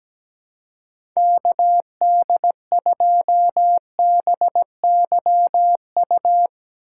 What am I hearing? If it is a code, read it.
Morse code, heard as a keyed tone, about 17 wpm, 700 Hz. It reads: KD2BYU
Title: USA Call Signs - 4x Character Spacing - 17wpm